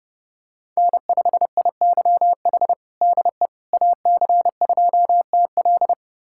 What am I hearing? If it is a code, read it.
Morse code, heard as a keyed tone, about 30 wpm, 700 Hz. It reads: N5IYH DE AC2TL
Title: USA Callsign DE Callsign - 5 Characters - 30wpm